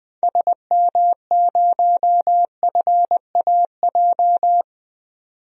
Morse code, heard as a keyed tone, about 20 wpm, 700 Hz. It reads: SM0FAJ
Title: DXCC-100 Call Signs - 6 Characters - 20wpm